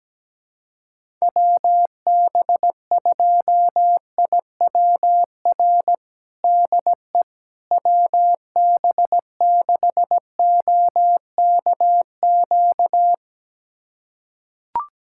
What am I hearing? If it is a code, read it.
Morse code, heard as a keyed tone, about 17 wpm, 700 Hz. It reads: WB2IWR DE WB6OKQ
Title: USA Callsign DE Callsign 17wpm